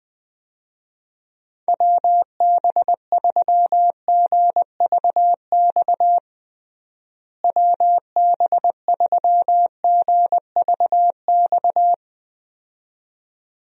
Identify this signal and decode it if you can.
Morse code, heard as a keyed tone, about 21 wpm, 700 Hz. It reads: WB3GVX WB3GVX
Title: USA Call Signs - 6 Characters - Repeated 2x - 20wpm